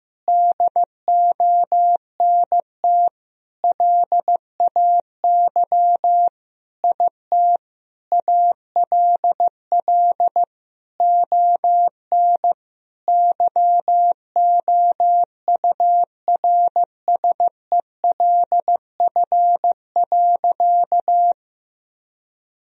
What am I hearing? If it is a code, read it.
Morse code, heard as a keyed tone, about 15 wpm, 700 Hz. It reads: DONT LAY IT ALL ON YOURSELF.
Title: Sentences from Top 1000 Words - Review Part 2 - 15wpm